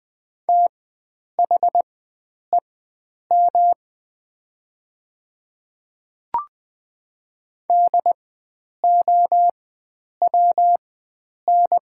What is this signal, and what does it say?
Morse code, heard as a keyed tone, about 20 wpm, spaced as for 9 wpm, 700 Hz. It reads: THEM DOWN
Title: 4x Character Spacing - Top 100 Words - 20wpm